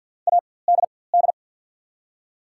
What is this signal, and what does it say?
Morse code, heard as a keyed tone, about 54 wpm, spaced as for 24 wpm, 690 Hz. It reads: ADD
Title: 4x Character Spacing - Top 300 Words - 50wpm